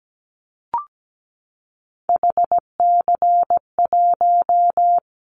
Morse code, heard as a keyed tone, about 17 wpm, 700 Hz. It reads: HC1
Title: Callsign Prefixes for Top 100 DXCC - 17wpm